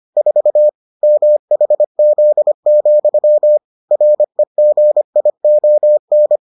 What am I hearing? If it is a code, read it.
Morse code, heard as a keyed tone, about 25 wpm, 600 Hz. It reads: 4 MHZ, REGION